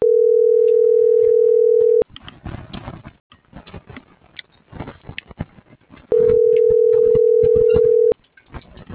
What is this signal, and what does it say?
Ringback tone (North America), 440 + 480 Hz.